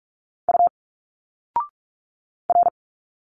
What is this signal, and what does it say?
Morse code, heard as a keyed tone, about 46 wpm, 700 Hz. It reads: U R
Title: Single Letters 45wpm